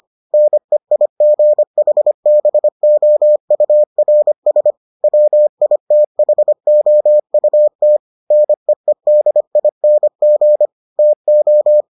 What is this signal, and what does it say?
Morse code, heard as a keyed tone, about 25 wpm, 600 Hz. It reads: NEIGHBOURS WITHOUT NEEDING TO